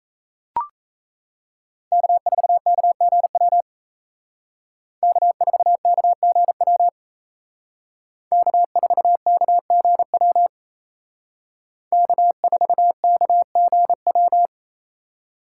Morse code, heard as a keyed tone, about 36 wpm, 700 Hz. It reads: K4KGW K4KGW K4KGW K4KGW